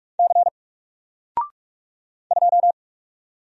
Morse code, heard as a keyed tone, about 46 wpm, 700 Hz. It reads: C 2
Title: Single Letter-Number Alt 45wpm